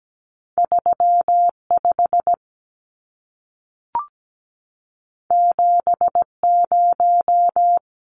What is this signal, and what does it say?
Morse code, heard as a keyed tone, about 17 wpm, 700 Hz. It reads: 35 70
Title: Numbers - 2-Digit 17wpm